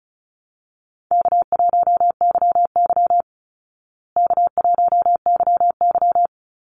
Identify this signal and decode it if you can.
Morse code, heard as a keyed tone, about 34 wpm, 700 Hz. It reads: K1YY K1YY
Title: Easy Call Signs Repeated 2x - 35wpm